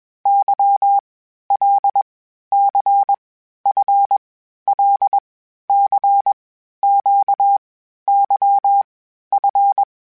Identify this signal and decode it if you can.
Morse code, heard as a keyed tone, about 21 wpm, 800 Hz. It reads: Y L C F L C Q Y F
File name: Basic Sets / Basic Letter Groups Collective7 / LFCYQ@20